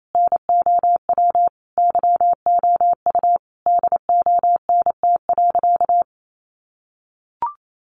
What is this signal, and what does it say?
Morse code, heard as a keyed tone, about 28 wpm, 700 Hz. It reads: NOW YOU DONT.
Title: Sentences from Top 600 Words 28wpm